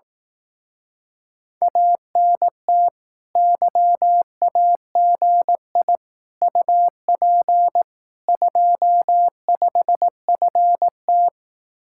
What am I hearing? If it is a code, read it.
Morse code, heard as a keyed tone, about 18 wpm, 700 Hz. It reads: ANT YAGI UP 25FT
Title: Speed-Racing Common QSO Phrases Encore - 15wpm